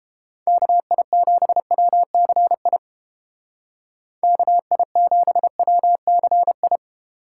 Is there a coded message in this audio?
Yes: Morse code from 0.5 to 6.8 s, heard as a keyed tone, about 32 wpm, 700 Hz. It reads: KI7WCI KI7WCI